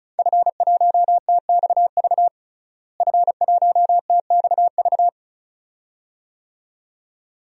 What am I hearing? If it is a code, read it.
Morse code, heard as a keyed tone, about 36 wpm, 700 Hz. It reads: F1TXV F1TXV
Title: DXCC-100 Call Signs - 5 Characters - Repeated 2x - 35wpm